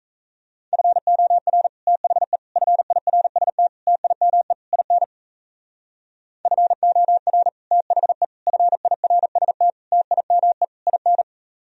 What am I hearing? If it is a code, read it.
Morse code, heard as a keyed tone, about 37 wpm, 700 Hz. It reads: FOR THE FIRST TIME IN FOR THE FIRST TIME IN